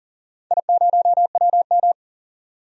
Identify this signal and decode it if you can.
Morse code, heard as a keyed tone, about 40 wpm, 700 Hz. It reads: I0WM
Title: DXCC-100 Call Signs - 4 Characters - 40wpm